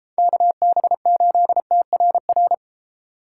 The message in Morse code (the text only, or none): KB8TRR